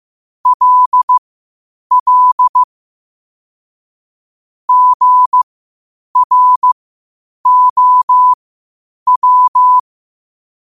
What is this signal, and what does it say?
Morse code, heard as a keyed tone, about 15 wpm, spaced as for 8 wpm, 1000 Hz. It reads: LL GROW
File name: qotdfiles / 8wpm